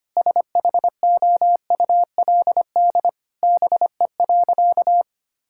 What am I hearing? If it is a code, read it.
Morse code, heard as a keyed tone, about 25 wpm, 700 Hz. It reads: SHOULD BE.
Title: Sentences from Top 1000 Words - Review Part 6 - 25wpm